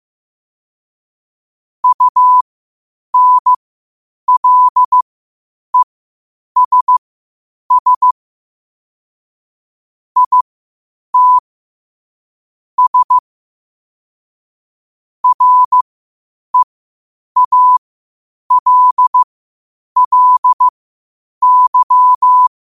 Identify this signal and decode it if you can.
Morse code, heard as a keyed tone, about 15 wpm, spaced as for 8 wpm, 1000 Hz. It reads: UNLESS IT S REALLY
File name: qotdfiles / 8wpm